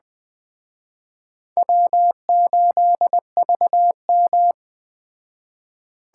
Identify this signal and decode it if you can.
Morse code, heard as a keyed tone, about 21 wpm, 700 Hz. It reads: W8VM
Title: Speed Racing - USA 1x2 Call Signs 17wpm